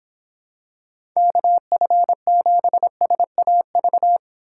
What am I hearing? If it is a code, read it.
Morse code, heard as a keyed tone, about 26 wpm, 700 Hz. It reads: KF7SAV